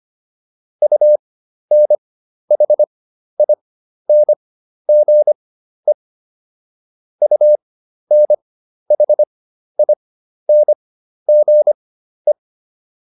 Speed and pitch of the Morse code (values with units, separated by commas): 25 wpm, spaced as for 12 wpm, 600 Hz